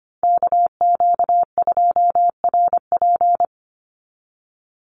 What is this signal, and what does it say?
Morse code, heard as a keyed tone, about 25 wpm, 700 Hz. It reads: KQ2RP
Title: NA SOTA Chaser Callsigns 25wpm